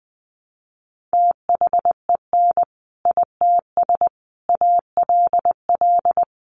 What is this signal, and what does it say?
Morse code, heard as a keyed tone, about 19 wpm, 700 Hz. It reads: THEN ITS ALL
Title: Sentences from Top 600 Words 20wpm